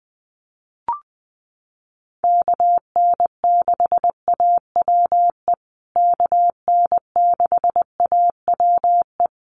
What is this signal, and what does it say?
Morse code, heard as a keyed tone, about 20 wpm, 700 Hz. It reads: KN6AWE KN6AWE
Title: USA Call Signs Repeated 3x 20wpm